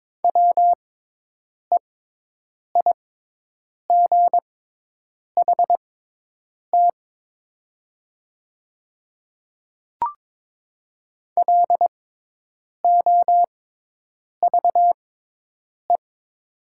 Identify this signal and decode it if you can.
Morse code, heard as a keyed tone, about 22 wpm, spaced as for 8 wpm, 700 Hz. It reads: WEIGHT LOVE
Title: 6x Character Spacing - Top 500 Words - 22wpm